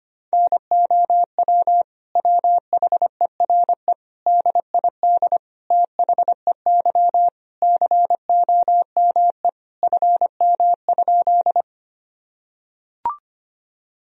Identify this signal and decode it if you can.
Morse code, heard as a keyed tone, about 25 wpm, 700 Hz. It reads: NOW WHERE DID THEY COME FM?
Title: Sentences from Top 200 Words and Abbreviations 25wpm